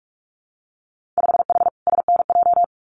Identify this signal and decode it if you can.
Morse code, heard as a keyed tone, about 46 wpm, 700 Hz. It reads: 5H SNJ
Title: Winter Field Day Exchanges Encore 45wpm